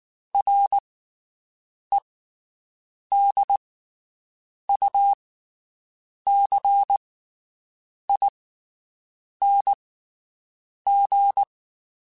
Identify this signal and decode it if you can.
Morse code, heard as a keyed tone, about 19 wpm, spaced as for 7 wpm, 800 Hz. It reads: REDUCING
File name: QSOs Spaced